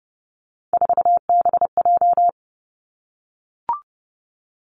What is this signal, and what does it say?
Morse code, heard as a keyed tone, about 30 wpm, 700 Hz. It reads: 4BJ
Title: Three Challenging Letter-Number Combinations 30wpm